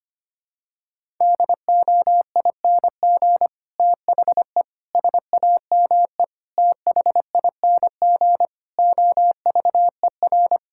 Morse code, heard as a keyed tone, about 25 wpm, 700 Hz. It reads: DOING THE SAME THING OVER
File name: Sets of 9 Words - 25wpm